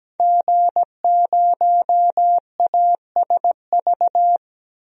Morse code, heard as a keyed tone, about 17 wpm, 700 Hz. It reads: G0ASV
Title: DXCC-100 Call Signs - 5 Characters - Repeated 3x - 17wpm